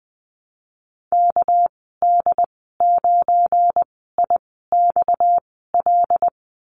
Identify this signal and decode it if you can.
Morse code, heard as a keyed tone, about 20 wpm, spaced as for 15 wpm, 700 Hz. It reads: KD9IXL
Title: 231 Lesson 37g - Letter x - Call Signs 20wpm